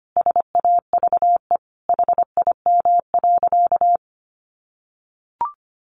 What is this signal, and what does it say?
Morse code, heard as a keyed tone, about 25 wpm, 700 Hz. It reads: SAVE HIM.
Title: Sentences from Top 1100 Words 25wpm